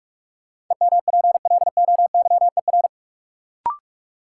Morse code, heard as a keyed tone, about 45 wpm, 700 Hz. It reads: EMPLOYER
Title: Sentences from Top 1600 Words - New Word Drill - 45wpm